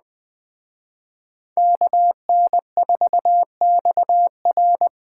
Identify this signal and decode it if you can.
Morse code, heard as a keyed tone, about 20 wpm, 700 Hz. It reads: KN4XR